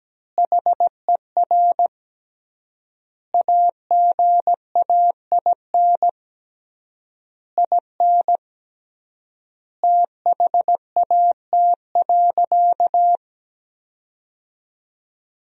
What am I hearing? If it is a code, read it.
Morse code, heard as a keyed tone, about 17 wpm, 700 Hz. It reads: HER AGAIN IN THAT.
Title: 3x Word-Spacing - Sentences from Top 200 Words - 17wpm